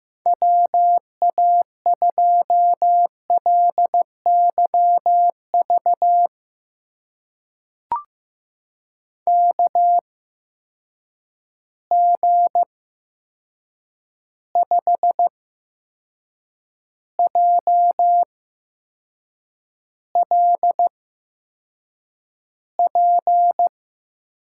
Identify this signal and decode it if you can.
Morse code, heard as a keyed tone, about 15 wpm, spaced as for 6 wpm, 700 Hz. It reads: WA2LYV KG5JLP
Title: USA Call Signs - 6 Characters - 8x Character Spacing - 15wpm